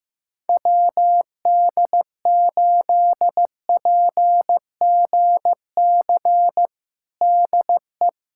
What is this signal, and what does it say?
Morse code, heard as a keyed tone, about 15 wpm, 700 Hz. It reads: WD8PGC DE
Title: USA Callsign DE Callsign - 6 Characters - 15wpm